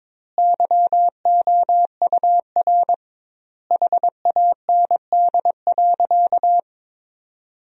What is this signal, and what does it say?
Morse code, heard as a keyed tone, about 22 wpm, 700 Hz. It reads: YOUR HAND.
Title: 2x Word-Spacing - Sentences from Top 400 Words - 22wpm